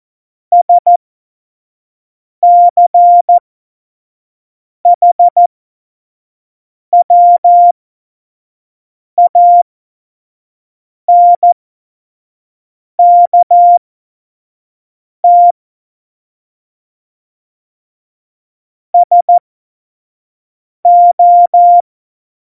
Morse code, heard as a keyed tone, about 14 wpm, spaced as for 5 wpm, 700 Hz. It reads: SCHWANKT SO
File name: Morsekurs / Witze / Witz 22. Mai 2018, 25 F70